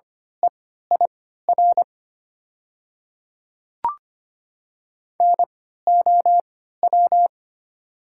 Morse code, heard as a keyed tone, about 25 wpm, spaced as for 14 wpm, 700 Hz. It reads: EIR NOW
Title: 3x Character Spacing - Top 100 Words - 25wpm